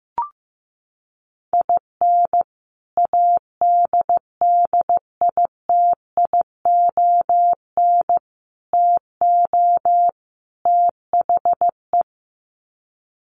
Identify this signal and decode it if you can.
Morse code, heard as a keyed tone, about 15 wpm, 700 Hz. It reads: IN ADDITION TO THE
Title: Sets of 4 Words 15wpm